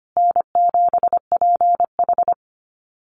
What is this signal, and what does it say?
Morse code, heard as a keyed tone, about 25 wpm, 700 Hz. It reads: N7PH
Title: USA Call Signs Repeated 3x 25wpm